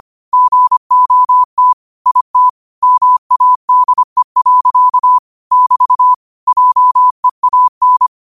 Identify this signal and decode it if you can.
Morse code, heard as a keyed tone, about 25 wpm, 1000 Hz. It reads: GOT IT MADE. <BT> JEAN
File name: qotdfiles / 25wpm